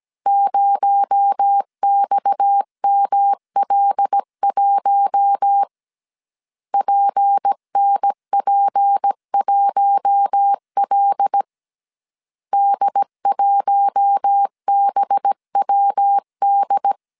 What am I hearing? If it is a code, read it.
Morse code, heard as a keyed tone, about 17 wpm, 800 Hz. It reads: 0XML1 PNP1L D1BWD
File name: CW / DET / 1 6 1500C S C 16wpm 2733